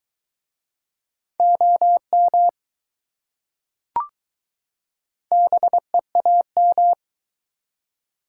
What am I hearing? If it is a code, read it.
Morse code, heard as a keyed tone, about 21 wpm, 700 Hz. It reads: OM BEAM